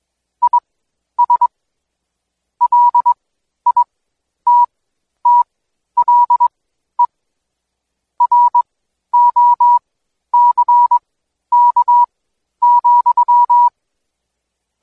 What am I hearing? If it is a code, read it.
Morse code, heard as a keyed tone, about 22 wpm, spaced as for 11 wpm, 970 Hz. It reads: IS LITTLE ROCK,
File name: Code Quick / AudioDownld2 / Seg120-Day 25-DayDrill 25